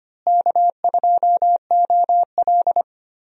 Morse code, heard as a keyed tone, about 25 wpm, 700 Hz. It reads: K2OL
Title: USA Call Signs - 4 Characters - Repeated 3x - 25wpm